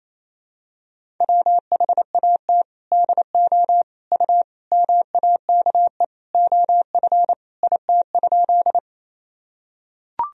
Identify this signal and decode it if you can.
Morse code, heard as a keyed tone, about 28 wpm, 700 Hz. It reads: WHAT DO U MAKE OF IT?
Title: Sentences from Top 100 Words and Abbreviations 28wpm